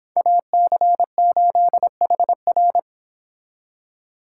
Morse code, heard as a keyed tone, about 29 wpm, 700 Hz. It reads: AC8HR